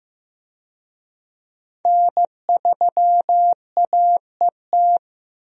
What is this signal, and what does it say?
Morse code, heard as a keyed tone, about 14 wpm, 700 Hz. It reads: N3AET
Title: USA Call Signs - 5 Characters - Repeated 2x - 15wpm